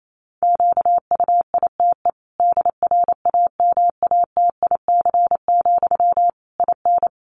Morse code, heard as a keyed tone, about 28 wpm, 700 Hz. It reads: QUITE DRAMATIC, IN
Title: Sentences from Top 1300 Words 28wpm